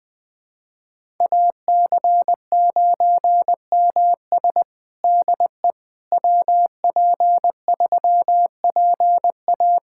Morse code, heard as a keyed tone, about 19 wpm, 700 Hz. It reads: AC9MS DE WP3PA
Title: USA Callsign DE Callsign - 5 Characters - 20wpm